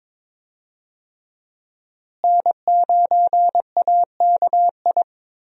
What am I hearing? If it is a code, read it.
Morse code, heard as a keyed tone, about 22 wpm, 700 Hz. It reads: N9AKI
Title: USA Call Signs - 5 Characters - Repeated 3x - 22wpm